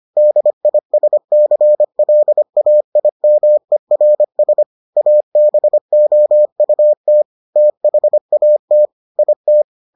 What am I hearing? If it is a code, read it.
Morse code, heard as a keyed tone, about 25 wpm, 600 Hz. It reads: DISCLAIMERS ABOUT THAT IT